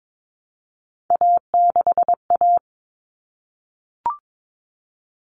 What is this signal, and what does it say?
Morse code, heard as a keyed tone, about 22 wpm, 700 Hz. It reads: A6A
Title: Common Cut Numbers - 3-Digit 22wpm